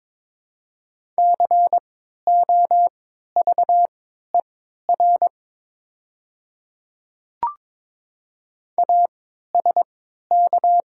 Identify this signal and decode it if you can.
Morse code, heard as a keyed tone, about 22 wpm, spaced as for 13 wpm, 700 Hz. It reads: COVER ASK
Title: 3x Character Spacing - Top 300 Words - 22wpm